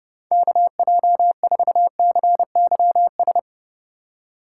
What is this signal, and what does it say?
Morse code, heard as a keyed tone, about 30 wpm, 700 Hz. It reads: KJ4CYS
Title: USA Call Signs - 2x Character Spacing - 30wpm